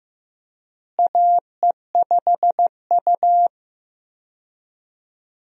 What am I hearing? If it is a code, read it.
Morse code, heard as a keyed tone, about 15 wpm, 700 Hz. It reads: AE5U
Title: USA Call Signs - 4 Characters - Repeated 2x - 15wpm